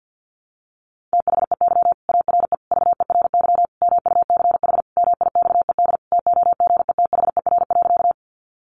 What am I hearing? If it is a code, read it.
Morse code, heard as a keyed tone, about 50 wpm, 700 Hz. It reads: THEY ARE VERY MUCH MIXED TOGETHER.
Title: Sentences from Top 1500 Words 50wpm